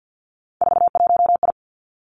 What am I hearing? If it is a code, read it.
Morse code, heard as a keyed tone, about 50 wpm, 700 Hz. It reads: 41I
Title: Three Challenging Letter-Number Combinations 50wpm